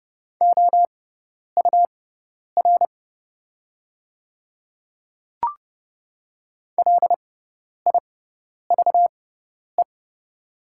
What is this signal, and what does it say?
Morse code, heard as a keyed tone, about 30 wpm, spaced as for 10 wpm, 700 Hz. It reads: OUR LIVE
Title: 6x Character Spacing - Top 200 Words - 30wpm